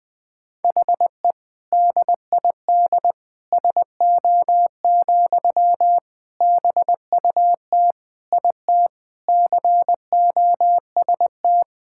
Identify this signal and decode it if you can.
Morse code, heard as a keyed tone, about 20 wpm, 700 Hz. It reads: HE DID SO, BUT IT COST
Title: Sentences from Top 1000 Words - Review Part 5 - 20wpm